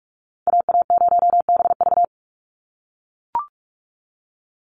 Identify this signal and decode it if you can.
Morse code, heard as a keyed tone, about 44 wpm, 700 Hz. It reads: AA0BV
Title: NA SOTA Activator Callsigns 45wpm